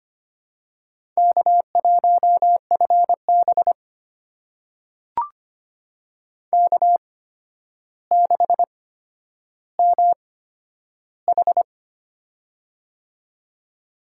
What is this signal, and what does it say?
Morse code, heard as a keyed tone, about 25 wpm, spaced as for 11 wpm, 700 Hz. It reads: K1FB K6MH
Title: USA Call Signs - 4 Characters - 8x Character Spacing - 25wpm